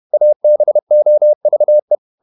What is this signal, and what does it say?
Morse code, heard as a keyed tone, about 31 wpm, 600 Hz. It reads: ABOVE